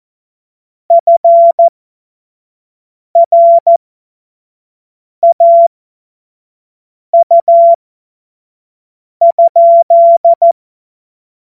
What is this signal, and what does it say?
Morse code, heard as a keyed tone, about 14 wpm, spaced as for 5 wpm, 680 Hz. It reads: FRAU?
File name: Morsekurs / Witze / Witz 15. Februar 2022 25 F70